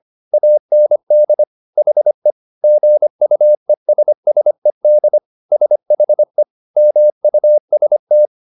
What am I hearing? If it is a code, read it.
Morse code, heard as a keyed tone, about 25 wpm, 600 Hz. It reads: AND HE GUESSED SHE MUST